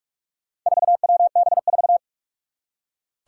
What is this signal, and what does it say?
Morse code, heard as a keyed tone, about 46 wpm, 700 Hz. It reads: 4WB4
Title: Four Challenging Letter-Number Combinations 45wpm